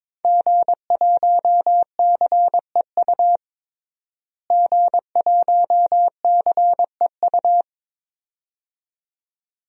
Morse code, heard as a keyed tone, about 22 wpm, 700 Hz. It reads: G1CEU G1CEU
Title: DXCC-100 Call Signs - 5 Characters - Repeated 2x - 22wpm